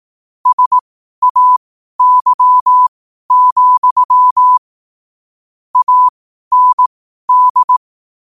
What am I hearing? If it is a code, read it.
Morse code, heard as a keyed tone, about 18 wpm, spaced as for 12 wpm, 1000 Hz. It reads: SAY, AND
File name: qotdfiles / 12wpm